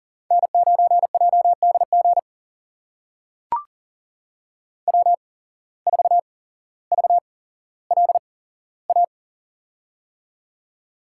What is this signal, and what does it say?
Morse code, heard as a keyed tone, about 40 wpm, spaced as for 17 wpm, 700 Hz. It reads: N9JDG W4VLA
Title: USA Call Signs - 5 Characters - 8x Character Spacing - 40wpm